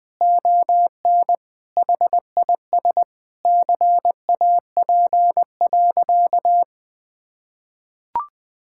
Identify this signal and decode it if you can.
Morse code, heard as a keyed tone, about 20 wpm, 700 Hz. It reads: ON HIS CAP.
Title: Sentences from Top 1600 Words 20wpm